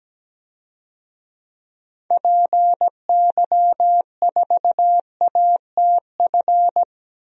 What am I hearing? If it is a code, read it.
Morse code, heard as a keyed tone, about 17 wpm, 700 Hz. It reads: PY4ATF
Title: DXCC-100 Call Signs - 6 Characters - 17wpm